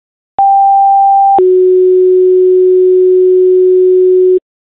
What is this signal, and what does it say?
Two-tone page, heard as a one-second tone, then a three-second tone, about 790 then 370 Hz. It is heard from 0.4 to 4.4 s.